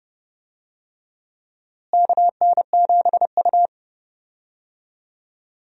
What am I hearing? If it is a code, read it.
Morse code, heard as a keyed tone, about 30 wpm, 700 Hz. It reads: KN7U
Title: USA Call Signs - 4 Characters - Repeated 3x - 30wpm